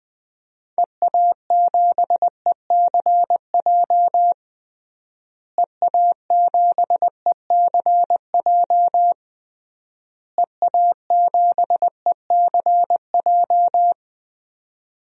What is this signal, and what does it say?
Morse code, heard as a keyed tone, about 20 wpm, 700 Hz. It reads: EA7ECJ EA7ECJ EA7ECJ
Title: DXCC-100 Call Signs - 6 Characters - Repeated 3x - 20wpm